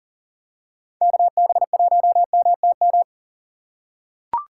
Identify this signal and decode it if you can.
Morse code, heard as a keyed tone, about 40 wpm, 700 Hz. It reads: KB1MTM
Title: USA Call Signs - 6 Characters - 2x Character Spacing - 40wpm